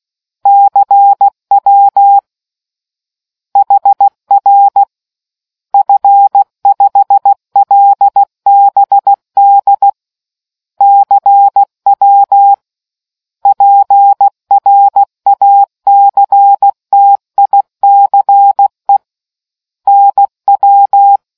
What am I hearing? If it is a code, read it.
Morse code, heard as a keyed tone, about 16 wpm, 800 Hz. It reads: CW HR F5LBD CW PRACTICE NW